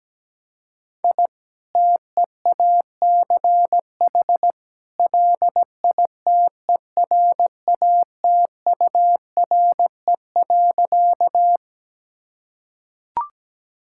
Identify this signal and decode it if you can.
Morse code, heard as a keyed tone, about 17 wpm, 700 Hz. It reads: I TEACH LITERATURE.
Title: Sentences from Top 1000 Words - Review Part 6 - 17wpm